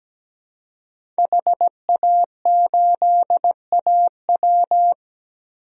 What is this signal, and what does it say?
Morse code, heard as a keyed tone, about 17 wpm, 700 Hz. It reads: HA8AW
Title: DXCC-100 Call Signs - 5 Characters - Repeated 3x - 17wpm